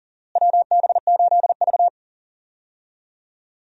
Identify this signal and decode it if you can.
Morse code, heard as a keyed tone, about 40 wpm, 700 Hz. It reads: WB8V
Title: USA Call Signs - 4 Characters - Repeated 3x - 40wpm